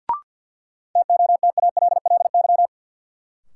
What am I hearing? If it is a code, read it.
Morse code, heard as a keyed tone, about 48 wpm, 700 Hz. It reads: TOTALLY
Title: Sentences from Top 1600 Words - New Word Drill Alt - 50wpm